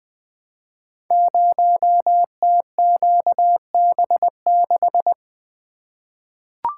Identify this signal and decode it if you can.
Morse code, heard as a keyed tone, about 20 wpm, 700 Hz. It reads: 0TQB6
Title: Code Groups - 5 Letter:Numbers - 20wpm